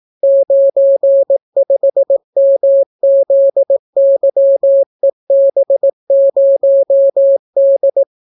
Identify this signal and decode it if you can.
Morse code, heard as a keyed tone, about 18 wpm, 550 Hz. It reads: 95MZYEB0D